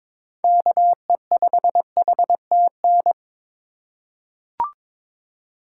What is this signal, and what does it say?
Morse code, heard as a keyed tone, about 22 wpm, 700 Hz. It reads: KE5HTN
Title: USA Call Signs - 6 Characters - 3x Character Spacing - 22wpm